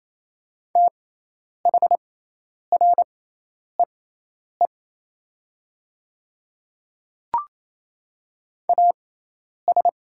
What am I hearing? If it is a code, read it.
Morse code, heard as a keyed tone, about 28 wpm, spaced as for 10 wpm, 690 Hz. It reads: THREE AS
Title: 6x Character Spacing - Top 200 Words - 28wpm